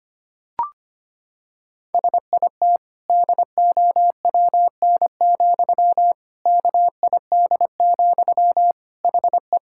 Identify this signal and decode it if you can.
Morse code, heard as a keyed tone, about 25 wpm, 700 Hz. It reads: SIT DOWN, KID, HE
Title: Sentences from Top 1600 Words 25wpm